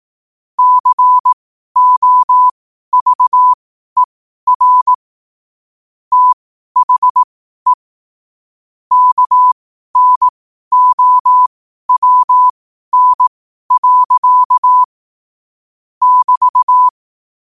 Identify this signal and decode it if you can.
Morse code, heard as a keyed tone, about 18 wpm, spaced as for 12 wpm, 1000 Hz. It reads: COVER THE KNOWN. <BT>
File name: qotdfiles / 12wpm